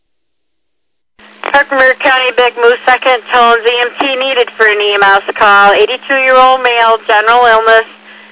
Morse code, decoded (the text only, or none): none